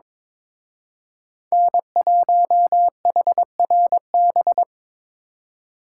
Morse code, heard as a keyed tone, about 26 wpm, 700 Hz. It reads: N1HRB
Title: Speed Racing - USA 1x3 Call Signs 22wpm